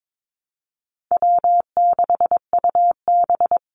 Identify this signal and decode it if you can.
Morse code, heard as a keyed tone, about 22 wpm, 700 Hz. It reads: W6UB
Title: NA SOTA Activator Callsigns 22wpm